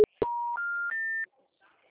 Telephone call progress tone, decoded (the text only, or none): special information tone